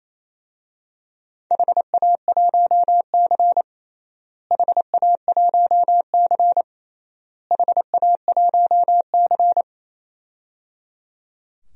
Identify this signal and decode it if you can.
Morse code, heard as a keyed tone, about 28 wpm, 700 Hz. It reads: HA1C HA1C HA1C
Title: DXCC-100 Call Signs - 4 Characters - Repeated 3x - 28wpm